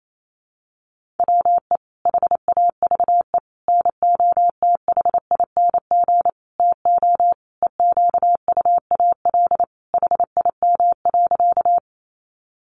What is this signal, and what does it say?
Morse code, heard as a keyed tone, about 28 wpm, 700 Hz. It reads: WE HAVE NOTHING TO EQUAL HIM.
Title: Sentences from Top 1400 Words 28wpm